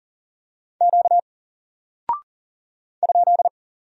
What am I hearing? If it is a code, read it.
Morse code, heard as a keyed tone, about 39 wpm, 700 Hz. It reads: Q ?
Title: Single Character (Letter:Number:Punctuation:Prosign) 40wpm